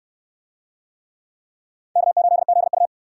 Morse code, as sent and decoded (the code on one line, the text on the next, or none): -.- ---.. --.. ..-
K8ZU